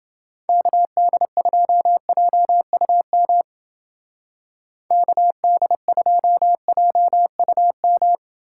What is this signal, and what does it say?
Morse code, heard as a keyed tone, about 29 wpm, 700 Hz. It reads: KD2JUM KD2JUM